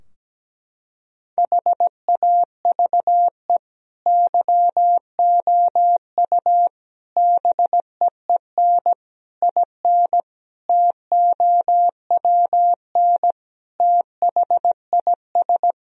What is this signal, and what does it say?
Morse code, heard as a keyed tone, about 17 wpm, 700 Hz. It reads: HAVE YOU BEEN IN TOWN THIS
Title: Sentences from Top 1000 Words - Review Part 2 - 17wpm